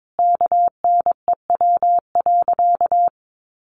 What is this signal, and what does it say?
Morse code, heard as a keyed tone, about 22 wpm, 700 Hz. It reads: KNEW.
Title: Sentences from Top 400 Words 22wpm